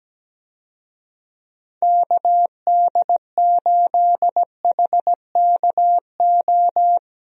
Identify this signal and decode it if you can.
Morse code, heard as a keyed tone, about 17 wpm, 700 Hz. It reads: KD8HKO
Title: USA Call Signs - 6 Characters - Repeated 3x - 17wpm